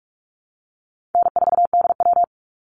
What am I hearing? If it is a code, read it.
Morse code, heard as a keyed tone, about 44 wpm, 700 Hz. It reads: N4DW
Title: Easy Call Signs Repeated 3x - 45wpm